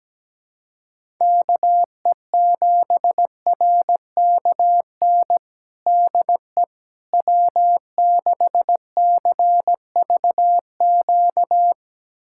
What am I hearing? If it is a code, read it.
Morse code, heard as a keyed tone, about 17 wpm, 700 Hz. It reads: KE7RKN DE W6CVQ
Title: USA Callsign DE Callsign 17wpm